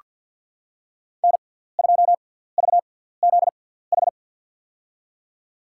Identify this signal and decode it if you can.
Morse code, heard as a keyed tone, about 49 wpm, spaced as for 17 wpm, 700 Hz. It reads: N2VZH
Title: USA Call Signs - 6x Character Spacing - 50wpm